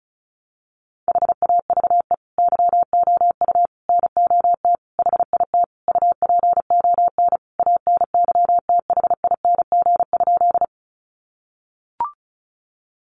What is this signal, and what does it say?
Morse code, heard as a keyed tone, about 34 wpm, 700 Hz. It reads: HAVE YOU NOT HIT UPON ANYTHING?
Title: Sentences from Top 1200 Words 35wpm